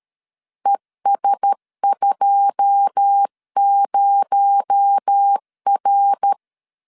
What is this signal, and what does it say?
Morse code, heard as a keyed tone, about 13 wpm, 800 Hz. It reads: ES20R
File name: LEZIONI / 9 28 12wpm 2351